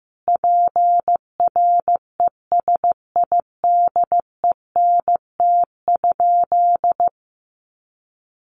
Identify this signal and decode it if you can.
Morse code, heard as a keyed tone, about 15 wpm, 700 Hz. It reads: PRESIDENT?
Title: Sentences from Top 1300 Words 15wpm